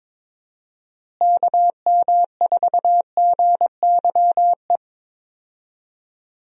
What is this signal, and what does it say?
Morse code, heard as a keyed tone, about 22 wpm, 700 Hz. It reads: KM4GYE
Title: USA Call Signs - 6 Characters - Repeated 3x - 22wpm